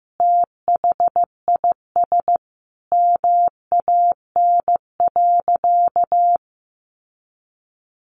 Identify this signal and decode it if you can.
Morse code, heard as a keyed tone, about 15 wpm, 700 Hz. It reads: THIS MAN.
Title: Sentences from Top 600 Words 15wpm